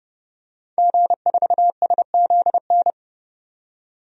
Morse code, heard as a keyed tone, about 30 wpm, 700 Hz. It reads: G4SZN
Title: DXCC-100 Call Signs - 5 Characters - 30wpm